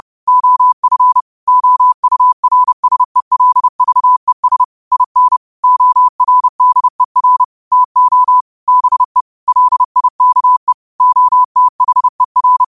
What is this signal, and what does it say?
Morse code, heard as a keyed tone, about 30 wpm, 1000 Hz. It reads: OF OURSELVES IN ORDER TO BE LIKE OTHER